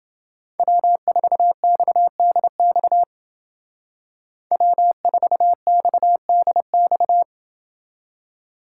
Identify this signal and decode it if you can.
Morse code, heard as a keyed tone, about 29 wpm, 700 Hz. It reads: W4XDX W4XDX